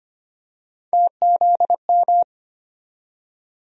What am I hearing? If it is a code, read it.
Morse code, heard as a keyed tone, about 25 wpm, 700 Hz. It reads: TZM
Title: USA Call Sign Postfixes - 3 Letters 25wpm